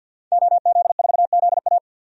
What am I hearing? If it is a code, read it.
Morse code, heard as a keyed tone, about 53 wpm, 700 Hz. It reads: OZ4ZA
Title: DXCC-100 Call Signs - 5 Characters - Repeated 2x - 50wpm